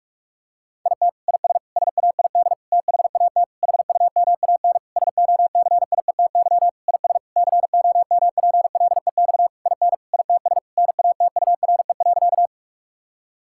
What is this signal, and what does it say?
Morse code, heard as a keyed tone, about 46 wpm, 700 Hz. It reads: IT IS SAID THAT HUMAN SOCIETY IS COMPLEX IN ITS NATURE.
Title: Sentences from Top 1000 Words - Review Part 6 - 45wpm